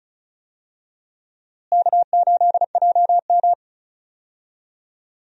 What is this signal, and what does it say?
Morse code, heard as a keyed tone, about 36 wpm, 700 Hz. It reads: K8JM
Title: USA Call Signs - 4 Characters - Repeated 3x - 35wpm